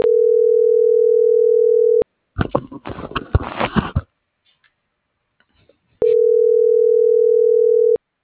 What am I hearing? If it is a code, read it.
Ringback tone (North America), 440 + 480 Hz.